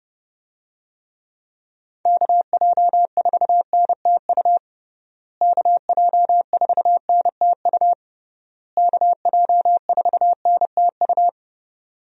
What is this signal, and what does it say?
Morse code, heard as a keyed tone, about 30 wpm, 700 Hz. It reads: KJ4NTU KJ4NTU KJ4NTU
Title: USA Call Signs - 6 Characters - Repeated 3x - 30wpm